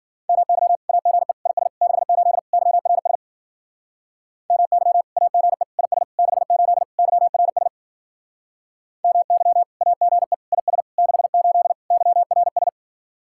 Morse code, heard as a keyed tone, about 49 wpm, 700 Hz. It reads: MY AGE IS 68 YRS MY AGE IS 68 YRS MY AGE IS 68 YRS
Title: Speed-Racing Common QSO Phrases - 40wpm